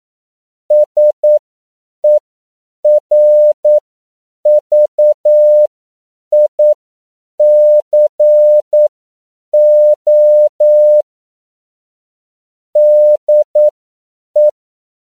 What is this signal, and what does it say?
Morse code, heard as a keyed tone, about 9 wpm, spaced as for 7 wpm, 600 Hz. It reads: SERVICO DE
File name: CW / 10PPM